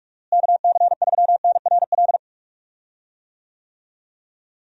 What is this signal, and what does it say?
Morse code, heard as a keyed tone, about 46 wpm, 700 Hz. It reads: KC3NRL
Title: USA Call Signs - 6 Characters - Repeated 3x - 45wpm